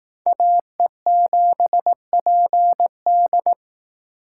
Morse code, heard as a keyed tone, about 19 wpm, 700 Hz. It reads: AE7PD